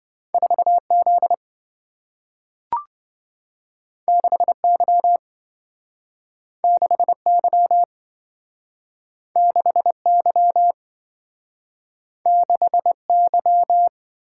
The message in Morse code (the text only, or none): 4Z 6Y 6Y 6Y 6Y